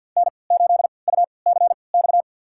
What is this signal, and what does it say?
Morse code, heard as a keyed tone, about 52 wpm, spaced as for 29 wpm, 700 Hz. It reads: N8UCX
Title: USA Call Signs - 5 Characters - 3x Character Spacing - 50wpm